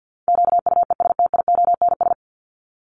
Morse code, heard as a keyed tone, about 50 wpm, 700 Hz. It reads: QUESTIONS
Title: ES:S Suffix - Top 100 Words with Base Words 50wpm